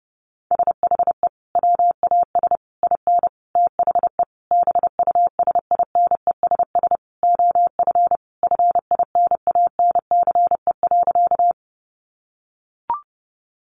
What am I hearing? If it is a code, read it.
Morse code, heard as a keyed tone, about 30 wpm, 700 Hz. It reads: SHE WAS IN THE BUSINESS OF FINANCE.
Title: Sentences from Top 1500 Words 30wpm